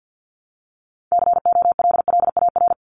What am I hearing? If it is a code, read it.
Morse code, heard as a keyed tone, about 50 wpm, 700 Hz. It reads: COLLAR
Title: Top 1751-2000 Words 50wpm